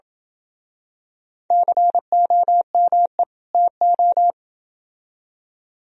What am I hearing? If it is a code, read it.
Morse code, heard as a keyed tone, about 28 wpm, 700 Hz. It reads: COME TO